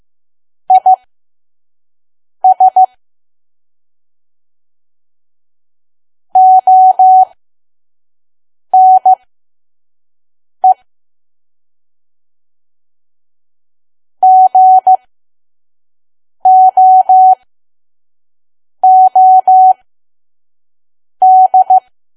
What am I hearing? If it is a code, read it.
Morse code, heard as a keyed tone, about 15 wpm, spaced as for 5 wpm, 750 Hz. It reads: IS ONE GOOD
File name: Archive / 5 WPM files / PM